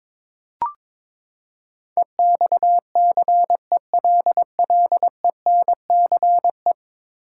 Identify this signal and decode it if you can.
Morse code, heard as a keyed tone, about 21 wpm, 700 Hz. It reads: EXCELLENCE
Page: morse-code-ninja-practice-22wpm